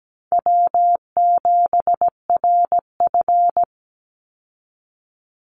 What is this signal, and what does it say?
Morse code, heard as a keyed tone, about 17 wpm, 700 Hz. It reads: W7RF
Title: Easy Call Signs Repeated 3x - 17wpm